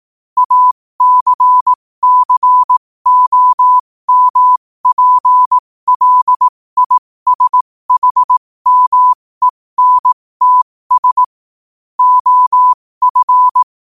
Morse code, heard as a keyed tone, about 18 wpm, spaced as for 15 wpm, 1000 Hz. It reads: ACCOMPLISHMENTS OF
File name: qotdfiles / 15wpm